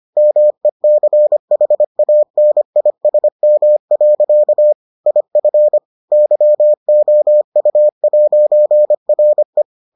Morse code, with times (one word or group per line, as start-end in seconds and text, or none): 0.2-4.7 MECHANISM.
5.1-5.8 IF
6.1-9.6 YOU'RE